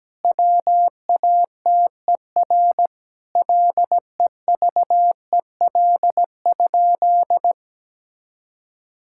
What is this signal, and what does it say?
Morse code, heard as a keyed tone, about 17 wpm, 700 Hz. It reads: WATER LEVEL?
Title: Sentences from Top 1000 Words - Review Part 5 - 17wpm